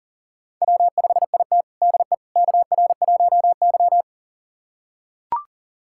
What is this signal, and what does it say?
Morse code, heard as a keyed tone, about 40 wpm, 700 Hz. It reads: W5IT DE KR1Y
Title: USA Callsign DE Callsign - 4 Characters - 40wpm